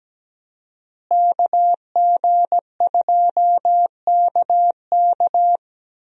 Morse code, heard as a keyed tone, about 17 wpm, 700 Hz. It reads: KG2KK
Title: USA Call Signs - 5 Characters - Repeated 3x - 17wpm